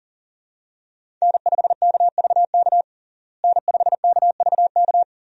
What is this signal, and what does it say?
Morse code, heard as a keyed tone, about 40 wpm, 700 Hz. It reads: N5KVK N5KVK
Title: USA Call Signs - 5 Characters - Repeated 3x - 40wpm